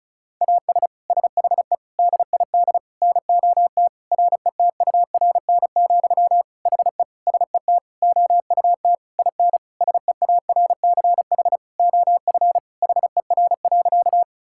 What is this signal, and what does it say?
Morse code, heard as a keyed tone, about 34 wpm, 700 Hz. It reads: AS SHE DID NOT RETURN, HE SET OUT IN SEARCH OF HER.
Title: Sentences from Top 1000 Words - Review Part 5 - 35wpm